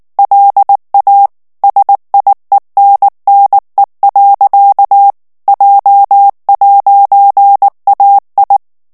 Morse code, heard as a keyed tone, about 19 wpm, 800 Hz. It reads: LA SIENNE. J'AI
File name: FRANCAIS TEXTE CLAIR 2 VITESSE 18 MOTS MINUTES